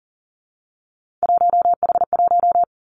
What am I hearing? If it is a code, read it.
Morse code, heard as a keyed tone, about 41 wpm, 700 Hz. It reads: 1H1
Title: Three Challenging Letter-Number Combinations 40wpm